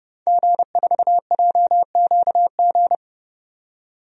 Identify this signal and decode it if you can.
Morse code, heard as a keyed tone, about 30 wpm, 700 Hz. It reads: G4JQG
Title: DXCC-100 Call Signs - 5 Characters - 30wpm